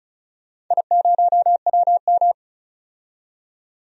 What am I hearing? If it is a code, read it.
Morse code, heard as a keyed tone, about 34 wpm, 700 Hz. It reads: I0WM
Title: DXCC-100 Call Signs - 4 Characters - 35wpm